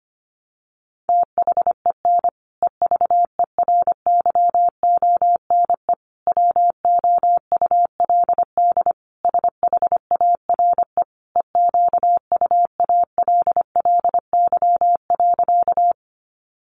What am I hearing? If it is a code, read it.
Morse code, heard as a keyed tone, about 25 wpm, 700 Hz. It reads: THEN EVERYONE WOULD SHARE EQUALLY.
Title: Sentences from Top 1400 Words 25wpm